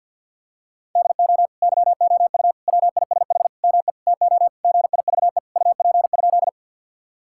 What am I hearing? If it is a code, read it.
Morse code, heard as a keyed tone, about 50 wpm, 700 Hz. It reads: DO YOU WISH ME TO GIVE UP?
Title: Sentences from Top 1600 Words 50wpm